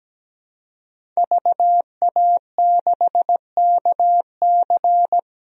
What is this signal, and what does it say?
Morse code, heard as a keyed tone, about 17 wpm, 700 Hz. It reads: VA6KC
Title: DXCC-100 Call Signs - 5 Characters - 17wpm